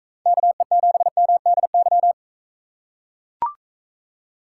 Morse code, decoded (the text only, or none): KE7MDY